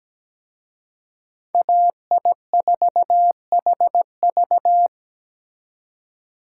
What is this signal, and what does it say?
Morse code, heard as a keyed tone, about 20 wpm, 700 Hz. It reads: AI4HV
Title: Speed Racing - USA 2x2 Call Signs 17wpm